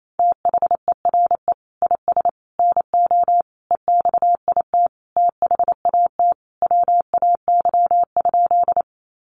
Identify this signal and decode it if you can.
Morse code, heard as a keyed tone, about 28 wpm, 700 Hz. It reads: THERE IS NO EXIT THAT WAY?
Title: Sentences from Top 1100 Words 28wpm